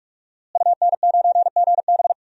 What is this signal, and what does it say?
Morse code, heard as a keyed tone, about 46 wpm, 700 Hz. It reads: UN9GB
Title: DXCC-100 Call Signs - 5 Characters - Repeated 2x - 45wpm